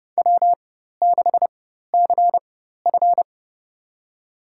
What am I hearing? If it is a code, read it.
Morse code, heard as a keyed tone, about 30 wpm, spaced as for 14 wpm, 700 Hz. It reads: W6CF
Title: USA Call Signs - 4 Characters - 4x Character Spacing - 30wpm